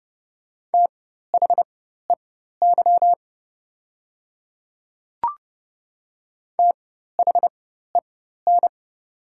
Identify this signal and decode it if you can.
Morse code, heard as a keyed tone, about 30 wpm, spaced as for 14 wpm, 700 Hz. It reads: THEY THEN
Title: 4x Character Spacing - Top 100 Words - 30wpm